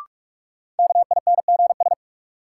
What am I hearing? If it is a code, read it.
Morse code, heard as a keyed tone, about 46 wpm, 700 Hz. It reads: KINGS
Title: Sentences from Top 1000 Words - Review Part 6 - New Word Drill - 45wpm